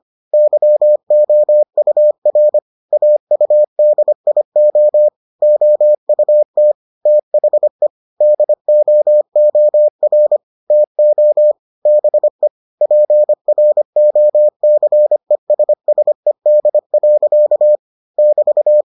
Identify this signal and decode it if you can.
Morse code, heard as a keyed tone, about 25 wpm, 600 Hz. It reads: YOUR AUDIO OUT THE DOOR TO BE PROCESSED. <BT>